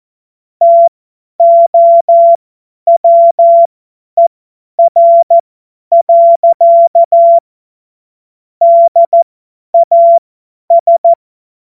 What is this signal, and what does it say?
Morse code, heard as a keyed tone, about 14 wpm, spaced as for 10 wpm, 680 Hz. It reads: TOWER. DAS
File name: Morsekurs / Witze / Witz 25. April 2023 50 F70